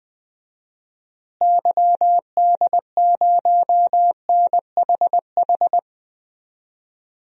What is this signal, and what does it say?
Morse code, heard as a keyed tone, about 20 wpm, 700 Hz. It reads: YD0NHH
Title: DXCC-100 Call Signs - 6 Characters - Repeated 2x - 20wpm